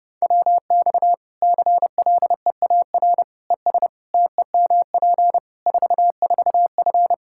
Morse code, heard as a keyed tone, about 30 wpm, 700 Hz. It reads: WX CLEAR ES TEMP 44F
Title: Practice QSO - Exchange 2 - Info - 30wpm